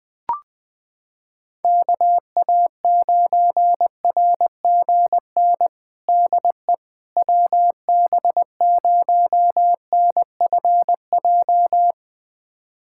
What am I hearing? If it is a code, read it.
Morse code, heard as a keyed tone, about 20 wpm, 700 Hz. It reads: KA9RGN DE WB0NFJ
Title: USA Callsign DE Callsign - 6 Characters - 20wpm